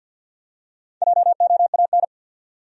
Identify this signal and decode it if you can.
Morse code, heard as a keyed tone, about 52 wpm, 700 Hz. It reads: JOAN
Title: USA Common Names - Max 4 Letters - 50wpm